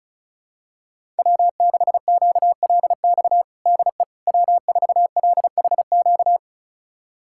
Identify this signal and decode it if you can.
Morse code, heard as a keyed tone, about 36 wpm, 700 Hz. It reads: W6QLX DE W4LHQ
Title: USA Callsign DE Callsign - 5 Characters - 35wpm